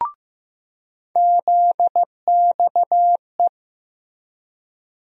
Morse code, heard as a keyed tone, about 15 wpm, 700 Hz. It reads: ZXE
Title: USA Call Sign Postfixes - 3 Letters 15wpm